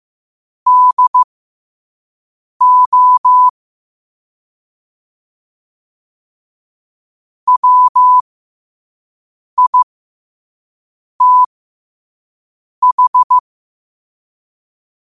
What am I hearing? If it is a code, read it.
Morse code, heard as a keyed tone, about 15 wpm, spaced as for 5 wpm, 1000 Hz. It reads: DO WITH
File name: qotdfiles / 5wpm